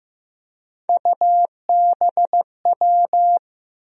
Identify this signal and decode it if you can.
Morse code, heard as a keyed tone, about 16 wpm, 700 Hz. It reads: UBW
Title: USA Call Sign Postfixes - 3 Letters 15wpm